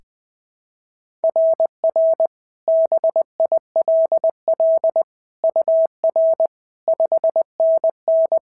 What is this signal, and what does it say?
Morse code, heard as a keyed tone, about 21 wpm, 660 Hz. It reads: RR BILL UR 5NN
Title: SOTA QSO Practice 20wpm